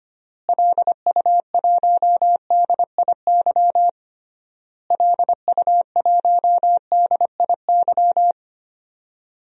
Morse code, heard as a keyed tone, about 25 wpm, 700 Hz. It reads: LU1DIY LU1DIY
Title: DXCC-100 Call Signs - 6 Characters - Repeated 3x - 25wpm